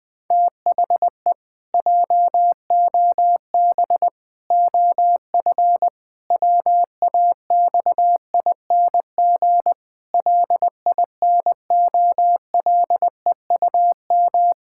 Morse code, heard as a keyed tone, about 20 wpm, 700 Hz. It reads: THE JOB OF WAXING LINOLEUM
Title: Pangram Challenge v2 - 20wpm